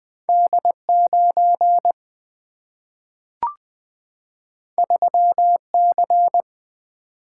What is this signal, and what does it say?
Morse code, heard as a keyed tone, about 20 wpm, 700 Hz. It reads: D9 3C
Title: Single-Letter Single-Number 20wpm